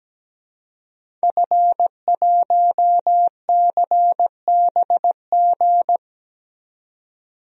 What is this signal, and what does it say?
Morse code, heard as a keyed tone, about 17 wpm, 700 Hz. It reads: F1CBG
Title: DXCC-100 Call Signs - 5 Characters - 17wpm